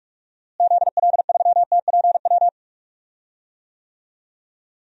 Morse code, heard as a keyed tone, about 43 wpm, 700 Hz. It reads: ZL3TPW
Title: DXCC-100 Call Signs - 6 Characters - Repeated 3x - 45wpm